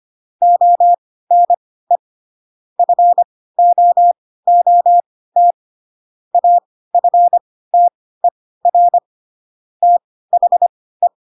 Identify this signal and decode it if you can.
Morse code, heard as a keyed tone, about 25 wpm, spaced as for 16 wpm, 700 Hz. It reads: ONE FOOT AFTER THE
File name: Pinocchio / Pinocchio ch3 10 25 ik1hge